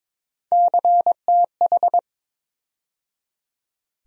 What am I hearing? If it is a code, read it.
Morse code, heard as a keyed tone, about 22 wpm, 700 Hz. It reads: CTH
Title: USA Call Sign Postfixes - 3 Letters 22wpm